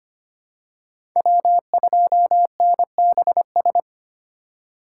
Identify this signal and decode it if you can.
Morse code, heard as a keyed tone, about 25 wpm, 700 Hz. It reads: W2NBS
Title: USA Call Signs - 5 Characters - 4x Character Spacing - 25wpm